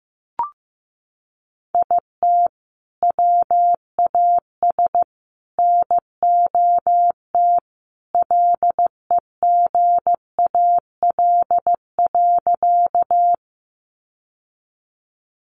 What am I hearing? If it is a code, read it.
Morse code, heard as a keyed tone, about 15 wpm, 700 Hz. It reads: IT WAS NOT LEGAL.
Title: Sentences from Top 1200 Words 15wpm